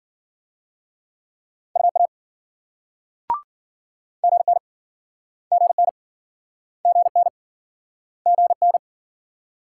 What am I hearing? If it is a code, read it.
Morse code, heard as a keyed tone, about 58 wpm, 700 Hz. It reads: UA GN GN GN GN